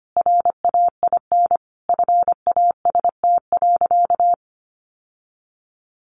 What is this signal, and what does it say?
Morse code, heard as a keyed tone, about 25 wpm, 700 Hz. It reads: RAIN FAST.
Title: Sentences from Top 800 Words 25wpm